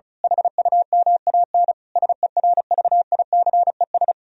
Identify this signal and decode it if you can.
Morse code, heard as a keyed tone, about 32 wpm, 700 Hz. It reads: HUMAN SERVICES
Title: Speed-Racing - Sets of 5 Words 35wpm